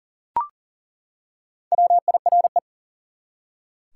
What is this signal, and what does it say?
Morse code, heard as a keyed tone, about 40 wpm, 700 Hz. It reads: WIRE
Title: Top 180 QSO Elements 40wpm